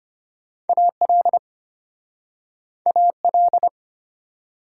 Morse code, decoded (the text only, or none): AL AL